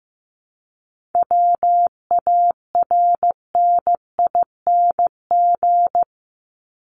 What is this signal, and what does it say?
Morse code, heard as a keyed tone, about 15 wpm, 700 Hz. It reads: WARNING
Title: Top 1751-2000 Words 15wpm